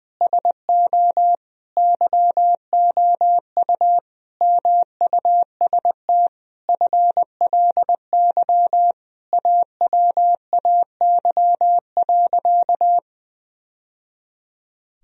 Morse code, heard as a keyed tone, about 20 wpm, 700 Hz. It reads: SO YOU MUST FLY AWAY.
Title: Sentences from Top 1000 Words - Review Part 2 - 20wpm